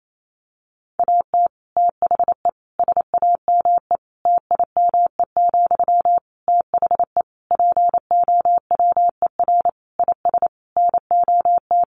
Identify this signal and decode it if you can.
Morse code, heard as a keyed tone, about 28 wpm, 700 Hz. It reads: AT THE SAME TIME, THE POWER IS NOT
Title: Sentences from Top 1500 Words 28wpm